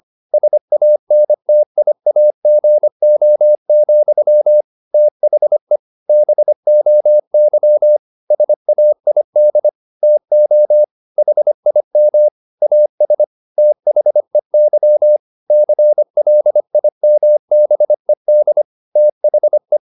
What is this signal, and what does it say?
Morse code, heard as a keyed tone, about 25 wpm, 600 Hz. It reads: SANTIAGO, THE BOY SAID TO HIM AS THEY CLIMBED THE